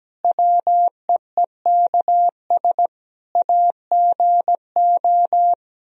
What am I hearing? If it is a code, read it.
Morse code, heard as a keyed tone, about 17 wpm, 700 Hz. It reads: WEEKS AGO
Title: Speed-Racing - Sets of 5 Words 17wpm